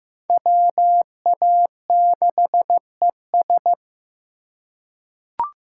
Morse code, heard as a keyed tone, about 15 wpm, 700 Hz. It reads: WA6ES
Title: USA Call Signs - 5 Characters - 4x Character Spacing - 15wpm